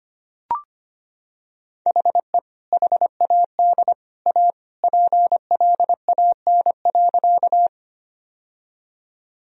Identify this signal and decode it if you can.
Morse code, heard as a keyed tone, about 25 wpm, 700 Hz. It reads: HE HAD A PLAN.
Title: Sentences from Top 1000 Words - Review Part 2 - 25wpm